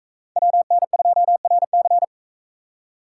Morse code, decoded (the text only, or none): WN2RC